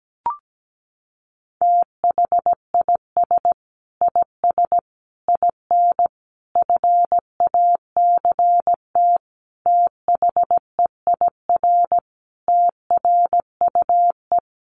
Morse code, heard as a keyed tone, about 17 wpm, 700 Hz. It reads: THIS IS IN FACT THEIR TRUE
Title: Sentences from Top 1200 Words 17wpm